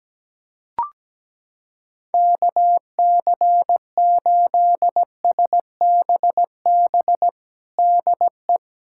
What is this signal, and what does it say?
Morse code, heard as a keyed tone, about 17 wpm, 700 Hz. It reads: KC8SBB DE
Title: USA Callsign DE Callsign - 6 Characters - 17wpm